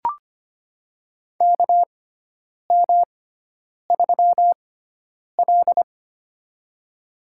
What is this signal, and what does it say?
Morse code, heard as a keyed tone, about 25 wpm, spaced as for 9 wpm, 700 Hz. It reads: KM3L
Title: USA Call Signs - 4 Characters - 6x Character Spacing - 25wpm